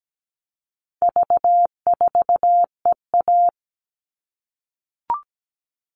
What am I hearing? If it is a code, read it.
Morse code, heard as a keyed tone, about 17 wpm, 700 Hz. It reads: V4EA
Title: All Cut Numbers - 4-Digit 17wpm